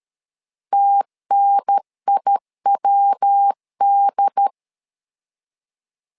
Morse code, heard as a keyed tone, about 13 wpm, 800 Hz. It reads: TNIWD
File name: LEZIONI / 3 MWRO / 3 MW 12wpm 1633